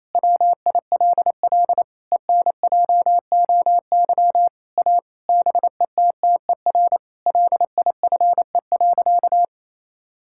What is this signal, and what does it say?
Morse code, heard as a keyed tone, about 28 wpm, 700 Hz. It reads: WILL ENJOY A BETTER LIFE.
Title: Sentences from Top 1000 Words - Review Part 5 - 28wpm